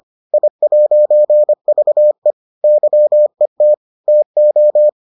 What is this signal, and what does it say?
Morse code, heard as a keyed tone, about 25 wpm, 600 Hz. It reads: I'VE YET TO